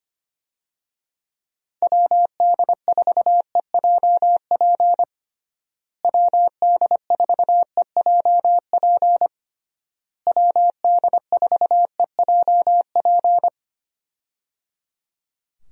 Morse code, heard as a keyed tone, about 25 wpm, 700 Hz. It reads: WD4EJP WD4EJP WD4EJP
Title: USA Call Signs - 6 Characters - Repeated 3x - 25wpm